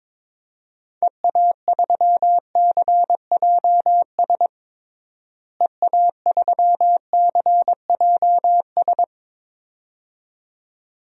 Morse code, heard as a keyed tone, about 22 wpm, 700 Hz. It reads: EA3CJS EA3CJS
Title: DXCC-100 Call Signs - 6 Characters - Repeated 2x - 22wpm